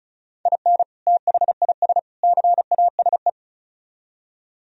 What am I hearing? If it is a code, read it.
Morse code, heard as a keyed tone, about 44 wpm, 700 Hz. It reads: IN THIS CASE